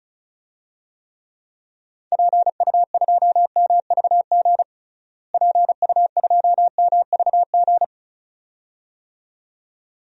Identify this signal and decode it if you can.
Morse code, heard as a keyed tone, about 34 wpm, 700 Hz. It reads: PU2MVG PU2MVG
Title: DXCC-100 Call Signs - 6 Characters - Repeated 2x - 35wpm